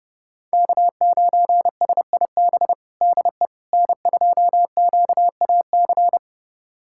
Morse code, heard as a keyed tone, about 32 wpm, 700 Hz. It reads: K9SIB DE N2QAC
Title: USA Callsign DE Callsign - 5 Characters - 30wpm